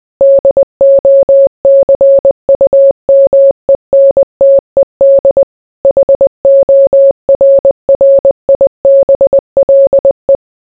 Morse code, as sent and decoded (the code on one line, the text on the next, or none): -.. --- -.-. ..- -- . -. - . -.. / .... --- .-. .-. .. -... .-.. .
DOCUMENTED HORRIBLE